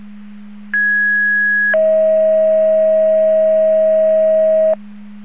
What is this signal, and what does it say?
Two-tone page, heard as a one-second tone, then a three-second tone, about 1650 then 640 Hz. It is heard from 0.7 to 4.7 s.
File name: Portage 154.385 Paging